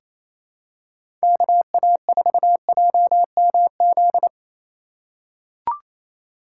Morse code, heard as a keyed tone, about 27 wpm, 700 Hz. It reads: KA4JMZ
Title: USA Call Signs - 6 Characters - 2x Character Spacing - 28wpm